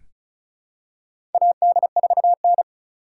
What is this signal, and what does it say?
Morse code, heard as a keyed tone, about 36 wpm, 700 Hz. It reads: AD4N
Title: USA Call Signs - 4x Character Spacing - 35wpm